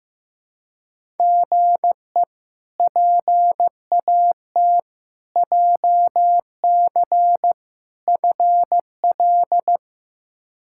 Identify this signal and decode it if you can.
Morse code, heard as a keyed tone, about 15 wpm, 700 Hz. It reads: GE PAT JC FL
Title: K1USN SST QSO Practice - 15wpm